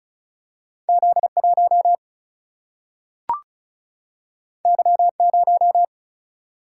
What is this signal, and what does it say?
Morse code, heard as a keyed tone, about 36 wpm, 700 Hz. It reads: Z 1 Y 0
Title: Single-Letter Single-Number 35wpm